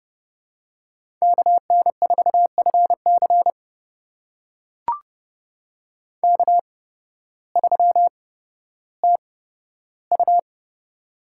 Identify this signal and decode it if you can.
Morse code, heard as a keyed tone, about 32 wpm, spaced as for 14 wpm, 700 Hz. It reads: KN4FC K3TU
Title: USA Call Signs - 8x Character Spacing - 30wpm